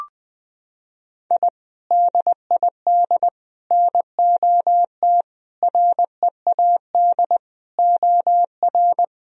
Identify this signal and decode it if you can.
Morse code, heard as a keyed tone, about 20 wpm, 700 Hz. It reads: I DID NOT READ OR
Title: Sentences from Top 1000 Words - Review Part 3 - 20wpm